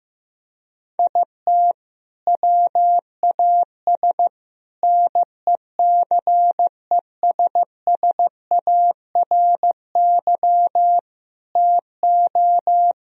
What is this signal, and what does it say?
Morse code, heard as a keyed tone, about 15 wpm, 700 Hz. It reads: IT WAS NECESSARY TO
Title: Sentences from Top 1000 Words - Review Part 1 - 15wpm